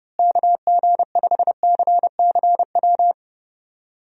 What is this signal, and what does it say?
Morse code, heard as a keyed tone, about 30 wpm, 700 Hz. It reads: KG5CCW
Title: USA Call Signs - 6 Characters - Repeated 3x - 30wpm